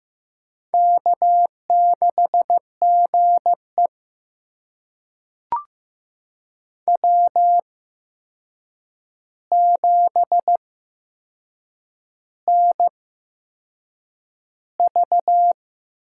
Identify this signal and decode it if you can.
Morse code, heard as a keyed tone, about 15 wpm, spaced as for 6 wpm, 700 Hz. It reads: K6GE W7NV
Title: USA Call Signs - 4 Characters - 8x Character Spacing - 15wpm